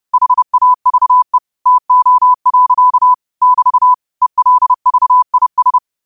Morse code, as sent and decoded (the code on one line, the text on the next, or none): .... .- ...- . / - --- .-.-.- / -...- / . .-.. ...- .. ...
HAVE TO. <BT> ELVIS